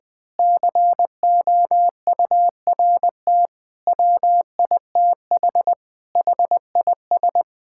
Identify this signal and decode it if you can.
Morse code, heard as a keyed tone, about 21 wpm, 700 Hz. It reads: COURT WITH HIS
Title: Sentences from Top 1600 Words 20wpm